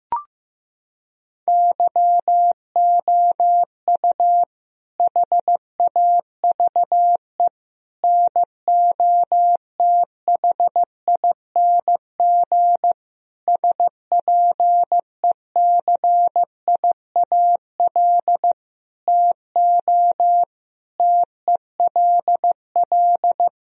Morse code, heard as a keyed tone, about 15 wpm, 700 Hz. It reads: YOU HAVE NOTHING SPECIAL TO TELL
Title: Sentences from Top 1000 Words - Review Part 3 - 15wpm